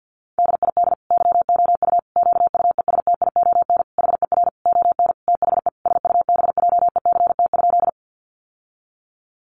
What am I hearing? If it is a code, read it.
Morse code, heard as a keyed tone, about 50 wpm, 700 Hz. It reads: DID YOU QUESTION HER ON THE SUBJECT?
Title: Sentences from Top 800 Words 50wpm